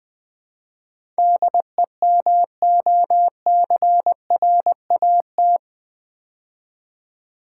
Morse code, heard as a keyed tone, about 20 wpm, 700 Hz. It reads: DEMOCRAT
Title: Build A Word - 4 Letters At A Time - 10-Letter Words - 20wpm